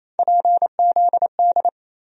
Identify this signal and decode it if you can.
Morse code, heard as a keyed tone, about 28 wpm, 700 Hz. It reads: PZD
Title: USA Call Sign Postfixes - 3 Letters 28wpm